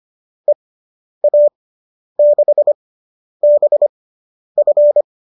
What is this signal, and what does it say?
Morse code, heard as a keyed tone, about 25 wpm, spaced as for 10 wpm, 590 Hz. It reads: EA6BF